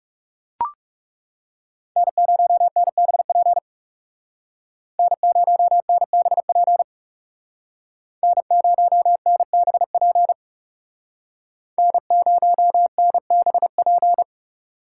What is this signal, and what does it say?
Morse code, heard as a keyed tone, about 38 wpm, 700 Hz. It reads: N0NBP N0NBP N0NBP N0NBP